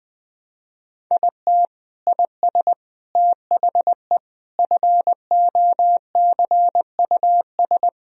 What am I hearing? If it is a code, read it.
Morse code, heard as a keyed tone, about 20 wpm, 700 Hz. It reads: IT IS THE FOCUS
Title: Sentences from Top 1000 Words - Review Part 4 - 20wpm